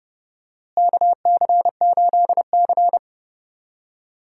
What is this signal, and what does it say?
Morse code, heard as a keyed tone, about 30 wpm, 700 Hz. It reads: KC8C
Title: USA Call Signs - 4 Characters - Repeated 3x - 30wpm